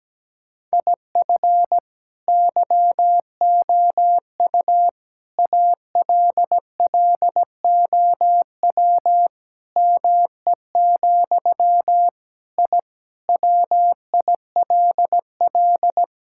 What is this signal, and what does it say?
Morse code, heard as a keyed tone, about 17 wpm, 700 Hz. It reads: IF YOU ALLOW ME, I WILL
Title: Sentences from Top 1000 Words - Review Part 5 - 17wpm